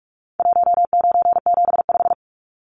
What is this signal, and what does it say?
Morse code, heard as a keyed tone, about 44 wpm, 700 Hz. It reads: 1975
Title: Numbers - 4-Digit 45wpm